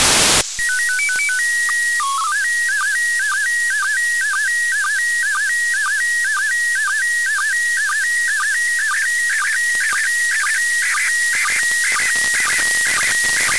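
SSTV (slow-scan television), PD 120.
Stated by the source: SSTV signal from the ISS received on 145.803 MHz.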